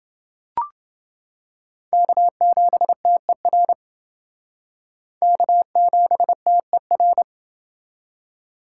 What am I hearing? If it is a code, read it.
Morse code, heard as a keyed tone, about 28 wpm, 700 Hz. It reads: K7TER K7TER